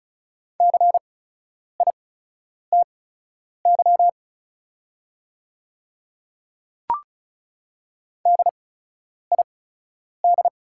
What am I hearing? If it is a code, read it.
Morse code, heard as a keyed tone, about 35 wpm, spaced as for 10 wpm, 690 Hz. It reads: CITY DID
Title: 8x Character Spacing - Top 500 Words - 35wpm